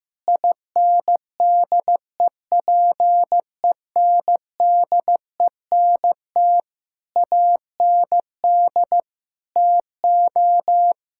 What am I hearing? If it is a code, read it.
Morse code, heard as a keyed tone, about 16 wpm, 700 Hz. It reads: INDEPENDENT AND TO
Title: Sentences from Top 1000 Words - Review Part 6 - 15wpm